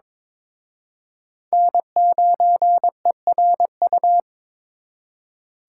Morse code, heard as a keyed tone, about 26 wpm, 700 Hz. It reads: N9ERU
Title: Speed Racing - USA 1x3 Call Signs 22wpm